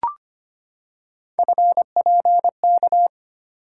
Morse code, heard as a keyed tone, about 25 wpm, 700 Hz. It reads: FPK
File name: Three Letter-Number Combinations - 25wpm